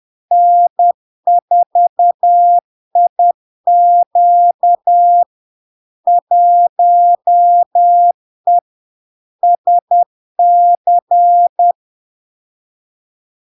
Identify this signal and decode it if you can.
Morse code, heard as a keyed tone, about 10 wpm, 700 Hz. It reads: N4IQ 1E SC
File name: LARK-FD-2022-Log-in-CW-10wpm-700-Hz